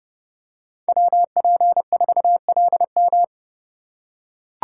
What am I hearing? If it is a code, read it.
Morse code, heard as a keyed tone, about 30 wpm, 700 Hz. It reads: WP4LM
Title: USA Call Signs - 5 Characters - 4x Character Spacing - 30wpm